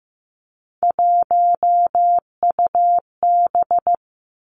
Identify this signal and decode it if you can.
Morse code, heard as a keyed tone, about 15 wpm, 700 Hz. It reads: 1UB
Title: Three Challenging Letter-Number Combinations 15wpm